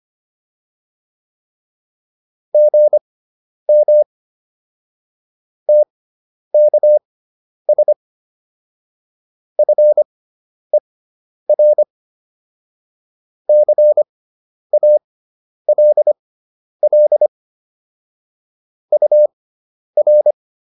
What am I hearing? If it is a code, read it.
Morse code, heard as a keyed tone, about 25 wpm, spaced as for 10 wpm, 600 Hz. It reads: GM TKS FER CALL UR